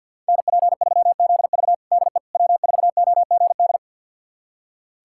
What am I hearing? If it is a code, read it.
Morse code, heard as a keyed tone, about 51 wpm, 700 Hz. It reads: NP3ZV DE W4OGD
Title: USA Callsign DE Callsign - 5 Characters - 50wpm